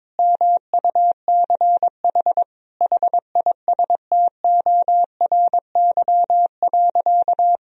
Morse code, heard as a keyed tone, about 22 wpm, 700 Hz. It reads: MUCH HISTORY.
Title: Sentences from Top 1000 Words - Review Part 2 - 22wpm